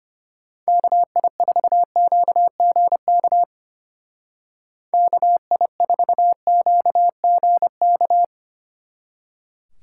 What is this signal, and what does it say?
Morse code, heard as a keyed tone, about 32 wpm, 700 Hz. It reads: KI4QGK KI4QGK